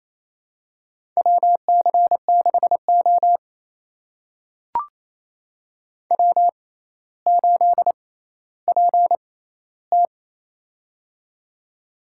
Morse code, heard as a keyed tone, about 28 wpm, spaced as for 14 wpm, 700 Hz. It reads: WC6O W8PT
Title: USA Call Signs - 4 Characters - 6x Character Spacing - 28wpm